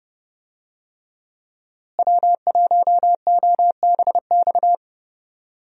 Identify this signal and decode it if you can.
Morse code, heard as a keyed tone, about 30 wpm, 700 Hz. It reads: W1OBX
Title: USA Call Signs - 5 Characters - Repeated 3x - 30wpm